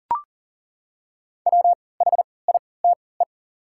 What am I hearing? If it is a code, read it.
Morse code, heard as a keyed tone, about 40 wpm, spaced as for 23 wpm, 700 Hz. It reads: WHITE
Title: 3x Character Spacing - Top 300 Words - 40wpm